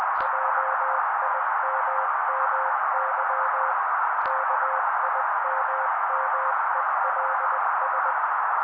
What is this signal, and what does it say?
Morse code, heard as a keyed tone, about 20 wpm, 530 Hz. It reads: JIMMY KIMMELS
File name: daily / 18wpm noise